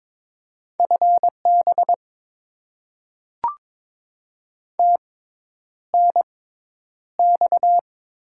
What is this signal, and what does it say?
Morse code, heard as a keyed tone, about 22 wpm, spaced as for 10 wpm, 700 Hz. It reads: FB TNX
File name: Top 20 QSO Elements - 6x Character Spacing - 22wpm